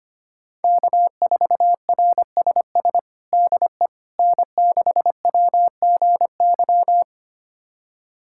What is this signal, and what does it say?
Morse code, heard as a keyed tone, about 25 wpm, 700 Hz. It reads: K4RSS DE N6WGY
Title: USA Callsign DE Callsign 25wpm